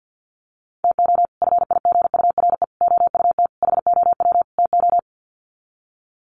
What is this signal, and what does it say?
Morse code, heard as a keyed tone, about 50 wpm, 700 Hz. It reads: TO FIGURE OUT HOW TO
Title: Sets of 5 Words 50wpm